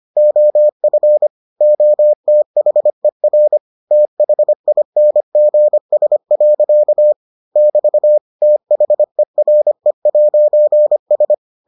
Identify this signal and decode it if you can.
Morse code, heard as a keyed tone, about 25 wpm, 600 Hz. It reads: OF OTHER THINGS. <BT> THERE'S